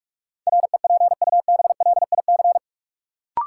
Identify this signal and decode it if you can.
Morse code, heard as a keyed tone, about 46 wpm, 700 Hz. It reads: REPUBLIC
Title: Sentences from Top 1500 Words - New Word Drill Alt - 45wpm